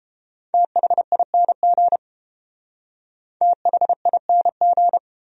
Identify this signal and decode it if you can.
Morse code, heard as a keyed tone, about 29 wpm, 700 Hz. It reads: THING THING